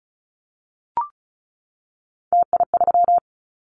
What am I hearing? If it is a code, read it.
Morse code, heard as a keyed tone, about 34 wpm, 700 Hz. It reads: TI3
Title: Callsign Prefixes for Top 100 DXCC - 35wpm